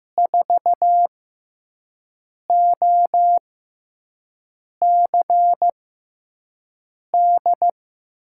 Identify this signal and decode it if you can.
Morse code, heard as a keyed tone, about 15 wpm, spaced as for 5 wpm, 700 Hz. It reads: 4OCD
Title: USA Call Signs - 6 Characters - 6x Character Spacing - 15wpm